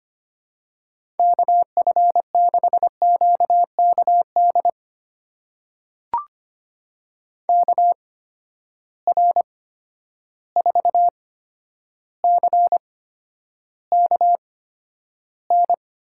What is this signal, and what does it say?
Morse code, heard as a keyed tone, about 25 wpm, spaced as for 11 wpm, 700 Hz. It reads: KF6QKD KR4CKN
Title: USA Call Signs - 6 Characters - 8x Character Spacing - 25wpm